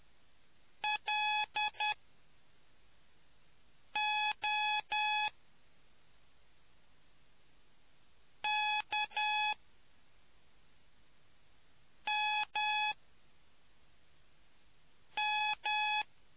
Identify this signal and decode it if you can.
Morse code, heard as a keyed tone, about 10 wpm, spaced as for 3 wpm, 850 Hz. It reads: LOKMM